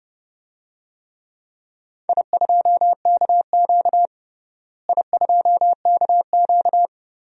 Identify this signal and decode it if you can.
Morse code, heard as a keyed tone, about 30 wpm, 700 Hz. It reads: I2KQ I2KQ
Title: DXCC-100 Call Signs - 4 Characters - Repeated 3x - 30wpm